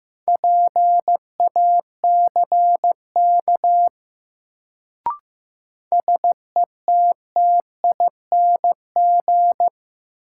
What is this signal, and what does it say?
Morse code, heard as a keyed tone, about 15 wpm, 700 Hz. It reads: PACK SETTING
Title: Sentences from Top 1300 Words - New Word Drill - 15wpm